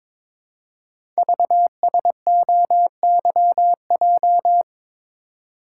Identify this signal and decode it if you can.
Morse code, heard as a keyed tone, about 22 wpm, 700 Hz. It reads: VSOYJ
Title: Code Groups - 5 Letters - 22wpm